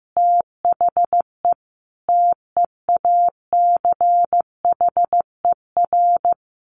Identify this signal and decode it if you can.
Morse code, heard as a keyed tone, about 15 wpm, 700 Hz. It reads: THE TEACHER
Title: Additional Easy Phrases 15wpm